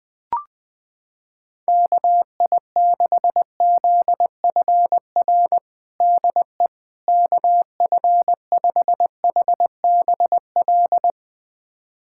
Morse code, heard as a keyed tone, about 20 wpm, 700 Hz. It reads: KI6ZFR DE KF5HBL
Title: USA Callsign DE Callsign - 6 Characters - 20wpm